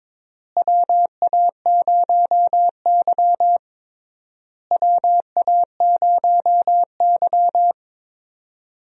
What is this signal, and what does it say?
Morse code, heard as a keyed tone, about 22 wpm, 700 Hz. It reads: WA0Y WA0Y
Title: USA Call Signs - 4 Characters - Repeated 3x - 22wpm